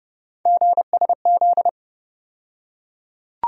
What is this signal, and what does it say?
Morse code, heard as a keyed tone, about 30 wpm, 700 Hz. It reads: GSZ
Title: USA Call Sign Postfixes - 3 Letters 30wpm